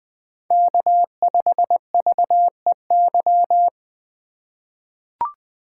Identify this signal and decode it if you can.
Morse code, heard as a keyed tone, about 20 wpm, 700 Hz. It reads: K5VEY
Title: USA Call Signs - 3x Character Spacing - 20wpm